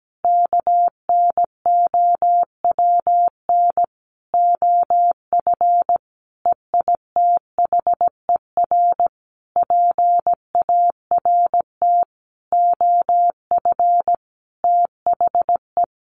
Morse code, heard as a keyed tone, about 17 wpm, 700 Hz. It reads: KNOWN OF EITHER PART OF THE
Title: Sentences from Top 1500 Words 17wpm